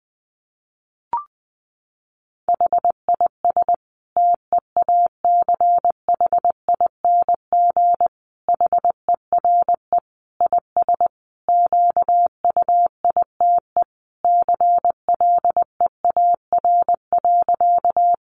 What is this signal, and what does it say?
Morse code, heard as a keyed tone, about 20 wpm, 700 Hz. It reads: HIS TEACHING HERE IS QUITE CLEAR.
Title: Sentences from Top 1100 Words 20wpm